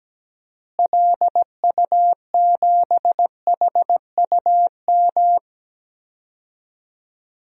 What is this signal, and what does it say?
Morse code, heard as a keyed tone, about 17 wpm, 700 Hz. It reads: LU7HUM
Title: DXCC-100 Call Signs - 6 Characters - Repeated 2x - 17wpm